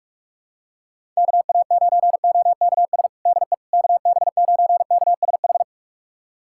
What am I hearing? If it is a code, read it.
Morse code, heard as a keyed tone, about 46 wpm, 690 Hz. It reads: KA9OKS DE KB9KSH
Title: USA Callsign DE Callsign - 6 Characters - 45wpm